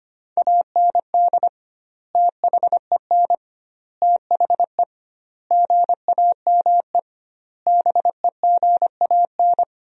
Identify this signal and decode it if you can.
Morse code, heard as a keyed tone, about 25 wpm, 700 Hz. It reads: AND THEN THE GAME BEGAN
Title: 2x Word-Spacing - Sentences from Top 500 Words - 25wpm